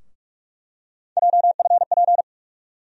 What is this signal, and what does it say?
Morse code, heard as a keyed tone, about 43 wpm, 700 Hz. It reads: JFP
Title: USA Call Sign Postfixes - 3 Letters 45wpm